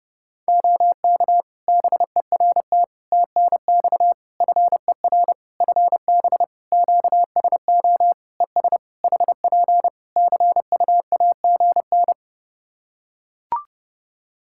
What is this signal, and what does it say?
Morse code, heard as a keyed tone, about 30 wpm, 700 Hz. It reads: OK BERT TNX FER FB QSO ES HP CUAGN
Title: Practice QSO - Exchange 4 - Ending - 30wpm